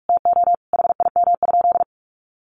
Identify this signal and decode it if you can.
Morse code, heard as a keyed tone, about 44 wpm, 700 Hz. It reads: TO HIM?
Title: Sentences from Top 1400 Words 45wpm